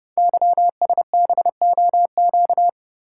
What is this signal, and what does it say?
Morse code, heard as a keyed tone, about 30 wpm, 700 Hz. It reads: YSBOQ
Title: Code Groups - 5 Letters - 30wpm